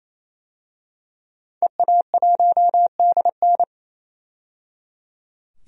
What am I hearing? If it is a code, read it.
Morse code, heard as a keyed tone, about 28 wpm, 700 Hz. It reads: EA1DN
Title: DXCC-100 Call Signs - 5 Characters - 28wpm